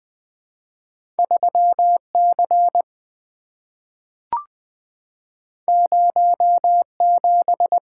Morse code, heard as a keyed tone, about 20 wpm, 700 Hz. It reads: 3C 07
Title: Two Letter-Number Combinations - 20wpm